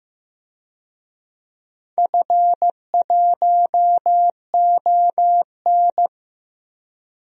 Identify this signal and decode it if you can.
Morse code, heard as a keyed tone, about 15 wpm, 700 Hz. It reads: F1ON
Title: DXCC-100 Call Signs - 4 Characters - 15wpm